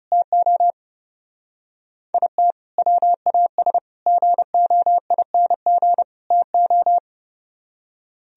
Morse code, heard as a keyed tone, about 36 wpm, 700 Hz. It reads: TO IT WAS GOING TO
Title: Speed-Racing - Sets of 4 Words 30wpm